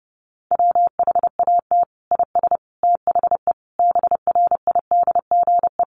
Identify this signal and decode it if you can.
Morse code, heard as a keyed tone, about 30 wpm, 700 Hz. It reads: WHAT IS THE BRIDGE
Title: Sentences from Top 1200 Words 30wpm